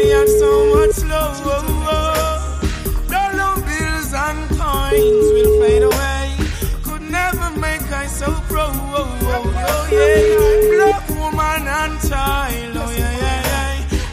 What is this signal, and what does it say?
Ringback tone (Europe, 425 Hz), with speech.